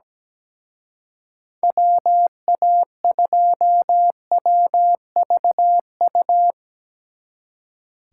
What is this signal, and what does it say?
Morse code, heard as a keyed tone, about 19 wpm, 700 Hz. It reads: WA2WVU
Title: Speed Racing - USA Call Signs 17wpm